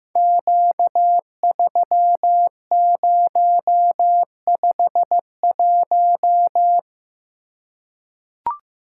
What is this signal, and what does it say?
Morse code, heard as a keyed tone, about 15 wpm, 700 Hz. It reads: Q3051
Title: Code Groups - 5 Letter:Numbers - 15wpm